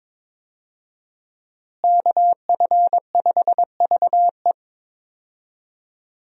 Morse code, heard as a keyed tone, about 22 wpm, 700 Hz. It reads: KF5VE
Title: USA Call Signs - 5 Characters - Repeated 2x - 22wpm